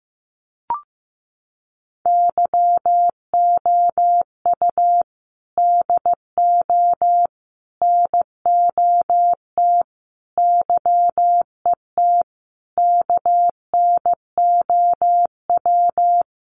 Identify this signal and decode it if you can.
Morse code, heard as a keyed tone, about 15 wpm, 700 Hz. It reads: YOU DO NOT YET KNOW
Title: Sentences from Top 1500 Words 15wpm